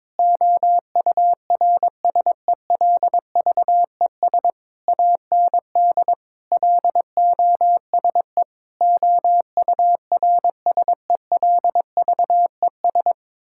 Morse code, heard as a keyed tone, about 22 wpm, 700 Hz. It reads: OURSELVES AND LOSE OURSELVES
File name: Difficult Sentences with Hints - 22wpm